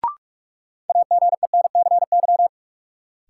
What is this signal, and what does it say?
Morse code, heard as a keyed tone, about 47 wpm, 700 Hz. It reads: AGENCY
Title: Sentences from Top 1300 Words - New Word Drill - 45wpm